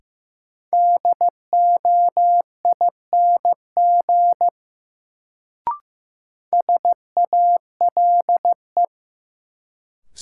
Morse code, heard as a keyed tone, about 16 wpm, 700 Hz. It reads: DOING SALE
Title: Sentences from Top 1400 Words - New Word Drill - 15wpm